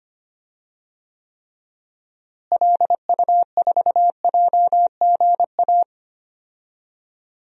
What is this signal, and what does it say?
Morse code, heard as a keyed tone, about 25 wpm, 700 Hz. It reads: LU4JGA
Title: DXCC-100 Call Signs - 6 Characters - Repeated 3x - 25wpm